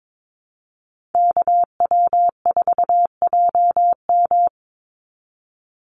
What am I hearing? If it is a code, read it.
Morse code, heard as a keyed tone, about 22 wpm, 700 Hz. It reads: KW4JM
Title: NA SOTA Activator Callsigns 22wpm